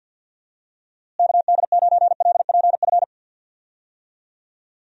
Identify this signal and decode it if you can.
Morse code, heard as a keyed tone, about 45 wpm, 700 Hz. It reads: KD9LPF
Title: USA Call Signs - 6 Characters - Repeated 2x - 50wpm